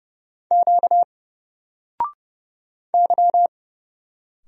Morse code, heard as a keyed tone, about 31 wpm, 700 Hz. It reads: Q Y
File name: Single-Letter-Number-30wpm